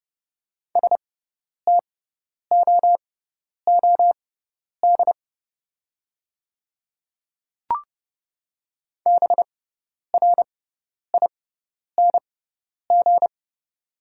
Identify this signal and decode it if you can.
Morse code, heard as a keyed tone, about 30 wpm, spaced as for 10 wpm, 690 Hz. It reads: STOOD BRING
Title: 6x Character Spacing - Top 500 Words - 30wpm